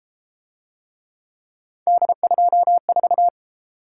Morse code, heard as a keyed tone, about 31 wpm, 700 Hz. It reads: D24